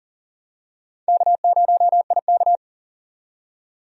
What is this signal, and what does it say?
Morse code, heard as a keyed tone, about 40 wpm, 700 Hz. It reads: K0IK
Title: USA Call Signs - 4 Characters - 40wpm